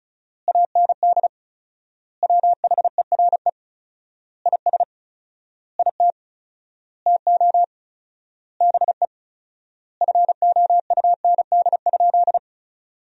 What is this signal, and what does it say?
Morse code, heard as a keyed tone, about 36 wpm, 700 Hz. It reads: AND WHERE IS IT TO BE FOUND?
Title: 4x Word-Spacing - Sentences from Top 300 Words - 35wpm